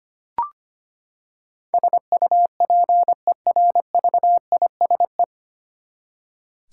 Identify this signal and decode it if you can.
Morse code, heard as a keyed tone, about 25 wpm, 700 Hz. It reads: SUPERVISE
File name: Nine Letter Words - 25wpm